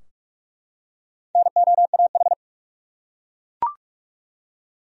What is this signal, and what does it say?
Morse code, heard as a keyed tone, about 46 wpm, 700 Hz. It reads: NOAH
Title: USA Common Names - 45wpm